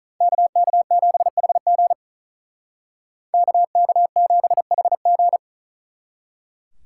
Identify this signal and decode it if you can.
Morse code, heard as a keyed tone, about 46 wpm, 700 Hz. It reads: KK7HG KK7HG